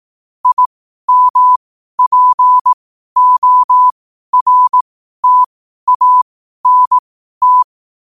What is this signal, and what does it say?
Morse code, heard as a keyed tone, about 18 wpm, spaced as for 12 wpm, 1000 Hz. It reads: IMPORTANT
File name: qotdfiles / 12wpm